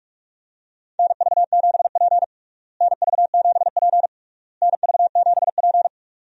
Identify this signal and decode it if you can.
Morse code, heard as a keyed tone, about 46 wpm, 700 Hz. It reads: NV7P NV7P NV7P
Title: USA Call Signs - 4 Characters - Repeated 3x - 45wpm